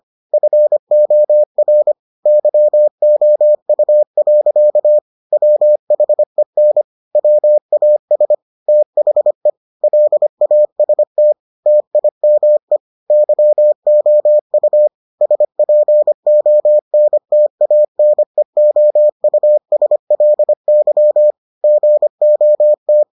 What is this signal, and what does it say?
Morse code, heard as a keyed tone, about 25 wpm, 600 Hz. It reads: FOR YOU. WHEN WAS THE LAST TIME YOU SPONTANEOUSLY GOT